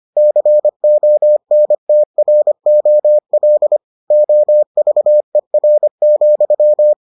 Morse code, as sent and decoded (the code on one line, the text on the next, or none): -.-. --- -. - .-. --- .-.. / --- ...- . .-. --..--
CONTROL OVER,